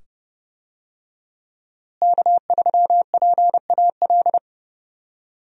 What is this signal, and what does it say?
Morse code, heard as a keyed tone, about 30 wpm, 700 Hz. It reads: K3PAL
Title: USA Call Signs - 5 Characters - Repeated 2x - 30wpm